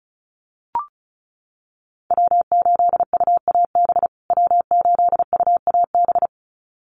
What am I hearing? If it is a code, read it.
Morse code, heard as a keyed tone, about 34 wpm, 700 Hz. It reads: W8UAB W8UAB
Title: USA Call Signs Repeated 2x 35wpm